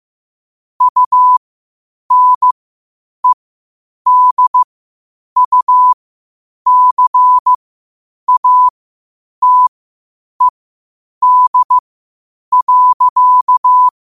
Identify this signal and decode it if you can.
Morse code, heard as a keyed tone, about 15 wpm, spaced as for 8 wpm, 1000 Hz. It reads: UNEDUCATED.
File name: qotdfiles / 8wpm